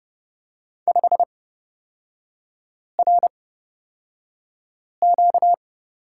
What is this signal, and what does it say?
Morse code, heard as a keyed tone, about 31 wpm, spaced as for 5 wpm, 700 Hz. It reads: 5RQ
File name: Lesson 38e - Letter q - Three Characters 30wpm